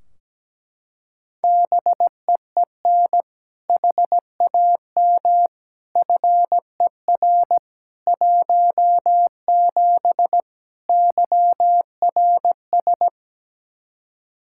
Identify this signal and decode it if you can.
Morse code, heard as a keyed tone, about 17 wpm, 700 Hz. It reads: BEEN HAM FER 17 YRS
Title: Practice QSO - Exchange 3 - More Info - 17wpm